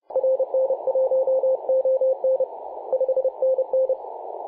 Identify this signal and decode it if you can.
Morse code, heard as a keyed tone, about 30 wpm, 520 Hz, with radio noise. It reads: RN1ON 5NN
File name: c21gc-24-895mhz-cw